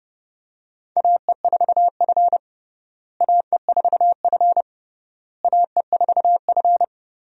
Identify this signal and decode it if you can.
Morse code, heard as a keyed tone, about 30 wpm, 700 Hz. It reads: AE4F AE4F AE4F
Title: USA Call Signs - 4 Characters - Repeated 3x - 30wpm